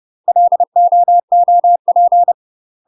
Morse code, heard as a keyed tone, about 30 wpm, 700 Hz. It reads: LOOP
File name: UK MASTER QSOs TRAINING / 30WPM / UK MASTER QSOs 0001